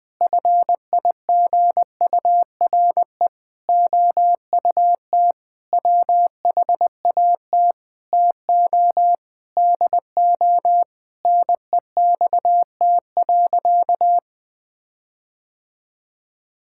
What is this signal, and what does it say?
Morse code, heard as a keyed tone, about 21 wpm, 700 Hz. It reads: FIGURE OUT WHAT TO DO NEXT.
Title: Sentences with 6-Word Phrases - Spoken Beginning - 20wpm